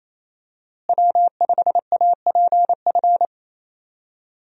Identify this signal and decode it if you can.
Morse code, heard as a keyed tone, about 27 wpm, 700 Hz. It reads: W5APF
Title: USA Call Signs - 5 Characters - Repeated 3x - 28wpm